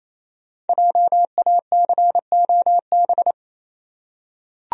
Morse code, heard as a keyed tone, about 28 wpm, 700 Hz. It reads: JACOB
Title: USA Common Names - 28wpm